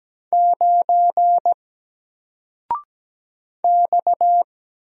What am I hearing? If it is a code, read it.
Morse code, heard as a keyed tone, about 17 wpm, 700 Hz. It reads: 9 X
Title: Single Letter-Number Alt 17wpm